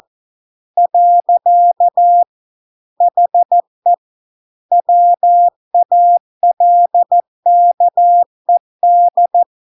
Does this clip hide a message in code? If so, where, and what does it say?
Yes: Morse code from 0.8 to 9.4 s, heard as a keyed tone, about 14 wpm, 700 Hz. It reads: . HE WALKED